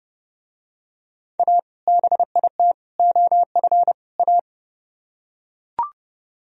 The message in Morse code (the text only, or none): A BIT OF A